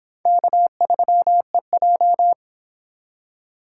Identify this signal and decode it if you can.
Morse code, heard as a keyed tone, about 29 wpm, 700 Hz. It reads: K3EJ